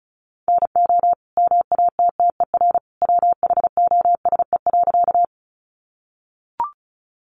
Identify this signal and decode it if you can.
Morse code, heard as a keyed tone, about 34 wpm, 700 Hz. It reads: NO MATTER WHOSE.
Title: Sentences from Top 1200 Words 35wpm